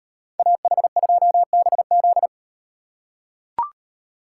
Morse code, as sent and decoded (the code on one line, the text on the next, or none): .- .... ..--- -... --..
AH2BZ